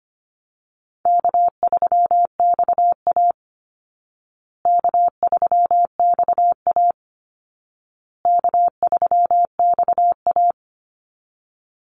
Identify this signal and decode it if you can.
Morse code, heard as a keyed tone, about 25 wpm, 700 Hz. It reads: K3XA K3XA K3XA
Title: Easy Call Signs Repeated 3x - 25wpm